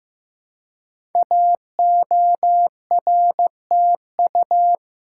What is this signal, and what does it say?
Morse code, heard as a keyed tone, about 15 wpm, 700 Hz. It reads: AORTU
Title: Code Groups - 5 Letters - 15wpm